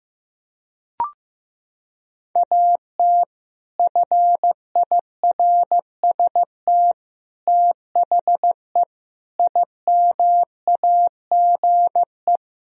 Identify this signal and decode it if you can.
Morse code, heard as a keyed tone, about 16 wpm, 700 Hz. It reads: AT FIRST THE IMAGE
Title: Sentences from Top 1000 Words - Review Part 4 - 15wpm